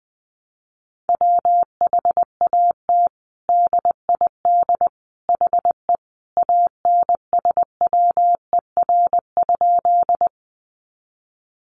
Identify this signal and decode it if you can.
Morse code, heard as a keyed tone, about 20 wpm, 700 Hz. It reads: WHAT DID HE ANSWER?
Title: Sentences from Top 300 Words 20wpm